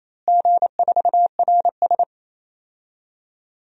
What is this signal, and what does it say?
Morse code, heard as a keyed tone, about 28 wpm, 700 Hz. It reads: G4RS
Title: DXCC-100 Call Signs - 4 Characters - Repeated 2x - 28wpm